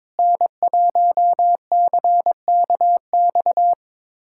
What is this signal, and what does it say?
Morse code, heard as a keyed tone, about 22 wpm, 700 Hz. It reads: N1CKX
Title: USA Callsign DE Callsign - 5 Characters - 22wpm